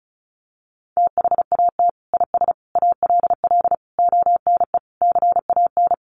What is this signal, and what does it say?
Morse code, heard as a keyed tone, about 34 wpm, 700 Hz. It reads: THAT IS ALL ONE CAN
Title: Sentences from Top 200 Words 35wpm